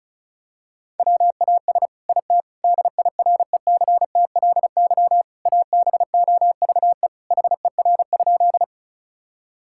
Morse code, heard as a keyed tone, about 36 wpm, 700 Hz. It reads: WAS IT DIRECTLY ABOVE HER?
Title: Sentences from Top 1000 Words - Review Part 5 - 35wpm